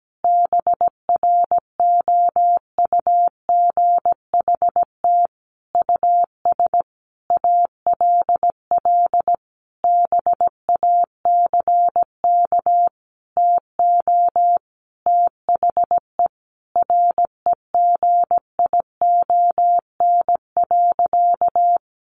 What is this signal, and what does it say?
Morse code, heard as a keyed tone, about 17 wpm, 700 Hz. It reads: BROUGHT US ALL BACK TO THE REGION.
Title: Sentences from Top 1500 Words 17wpm